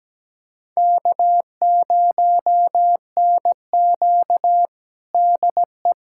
Morse code, heard as a keyed tone, about 17 wpm, 700 Hz. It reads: K0NQ DE
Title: USA Callsign DE Callsign - 4 Characters - 17wpm